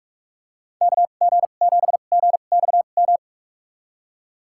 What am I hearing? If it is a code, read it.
Morse code, heard as a keyed tone, about 46 wpm, spaced as for 32 wpm, 700 Hz. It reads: KG7GXM
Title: USA Call Signs - 6 Characters - 2x Character Spacing - 45wpm